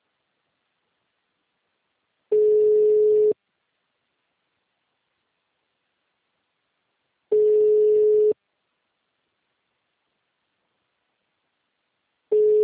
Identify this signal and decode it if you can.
Ringback tone (Europe, 425 Hz).
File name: out- -2021- - - .8435